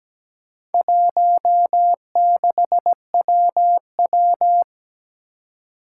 Morse code, heard as a keyed tone, about 17 wpm, 700 Hz. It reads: 16WW
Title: Four Challenging Letter-Number Combinations 17wpm